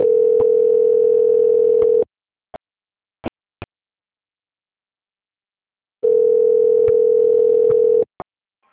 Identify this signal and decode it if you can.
Ringback tone (North America), 440 + 480 Hz.
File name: rg-1600-unknown- - - .12537